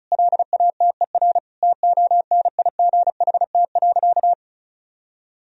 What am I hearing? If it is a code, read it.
Morse code, heard as a keyed tone, about 34 wpm, 700 Hz. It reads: LATER TONIGHT.
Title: Sentences from Top 1000 Words - Review Part 4 - 35wpm